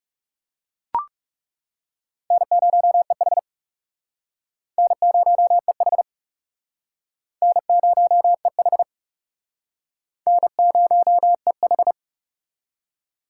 Morse code, heard as a keyed tone, about 36 wpm, 700 Hz. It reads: N0EH N0EH N0EH N0EH
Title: Speed Racing - USA 1x2 Call Signs 30wpm